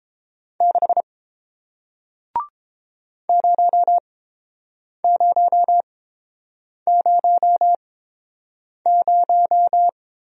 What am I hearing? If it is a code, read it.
Morse code, heard as a keyed tone, about 30 wpm, spaced as for 7 wpm, 700 Hz. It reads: 6 0000